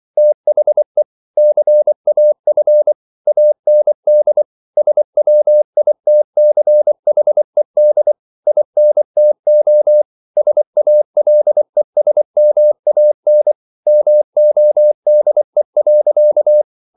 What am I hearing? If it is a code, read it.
Morse code, heard as a keyed tone, about 24 wpm, 600 Hz. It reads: THE CAF AND SWITCHED INTO SALESMAN MODE.